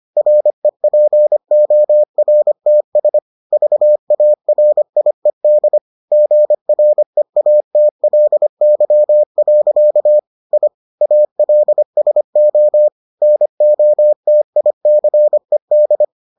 Morse code, heard as a keyed tone, about 25 wpm, 600 Hz. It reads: REPORTS VARIED GREATLY. I ALSO NOTICED